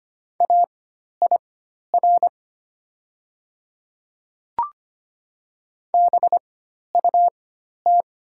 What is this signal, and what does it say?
Morse code, heard as a keyed tone, about 25 wpm, spaced as for 12 wpm, 690 Hz. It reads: AIR BUT
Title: 4x Character Spacing - Top 200 Words - 25wpm